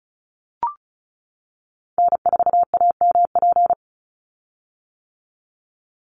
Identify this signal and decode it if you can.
Morse code, heard as a keyed tone, about 36 wpm, 700 Hz. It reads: N4AMP
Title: USA Call Signs 35wpm